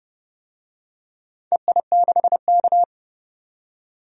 Morse code, heard as a keyed tone, about 30 wpm, 700 Hz. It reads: EI6K
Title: DXCC-100 Call Signs - 4 Characters - 30wpm